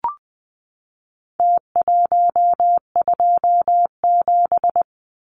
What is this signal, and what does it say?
Morse code, heard as a keyed tone, about 20 wpm, 700 Hz. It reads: T127
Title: Numbers - Contest Style - 4-Digit 20wpm